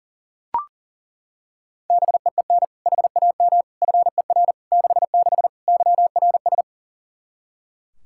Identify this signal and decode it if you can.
Morse code, heard as a keyed tone, about 40 wpm, 700 Hz. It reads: BEEN HAM FER 66 YRS
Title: Practice QSO - Exchange 3 - More Info - 40wpm